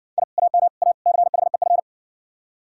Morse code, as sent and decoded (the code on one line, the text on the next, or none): .. / .- -- / .- / -.-. .... ..-.
I AM A CHEF